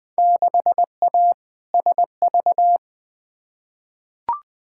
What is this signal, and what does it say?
Morse code, heard as a keyed tone, about 20 wpm, 700 Hz. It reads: 6A SV
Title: Field Day Exchanges Encore - 20wpm